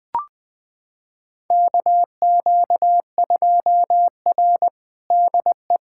Morse code, heard as a keyed tone, about 20 wpm, 700 Hz. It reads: KQ2R DE
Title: USA Callsign DE Callsign - 4 Characters - 20wpm